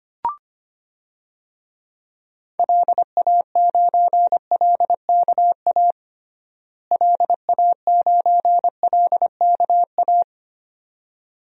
Morse code, heard as a keyed tone, about 25 wpm, 700 Hz. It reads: LA9LKA LA9LKA
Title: DXCC-100 Call Signs - 6 Characters - Repeated 2x - 25wpm